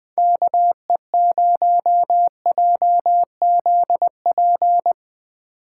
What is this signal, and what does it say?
Morse code, heard as a keyed tone, about 21 wpm, 700 Hz. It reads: KE0JZP
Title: Speed Racing - USA 2x3 Call Signs 20wpm